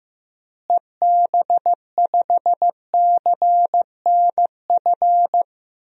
Morse code, heard as a keyed tone, about 15 wpm, 700 Hz. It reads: EB5CNF
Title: DXCC-100 Call Signs - 6 Characters - Repeated 2x - 15wpm